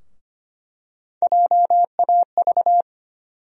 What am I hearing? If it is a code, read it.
Morse code, heard as a keyed tone, about 25 wpm, 700 Hz. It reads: JAV
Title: USA Call Sign Postfixes - 3 Letters 25wpm